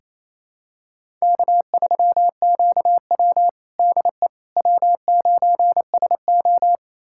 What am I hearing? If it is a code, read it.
Morse code, heard as a keyed tone, about 28 wpm, 700 Hz. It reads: K3QW DE W9SO
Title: USA Callsign DE Callsign - 4 Characters - 28wpm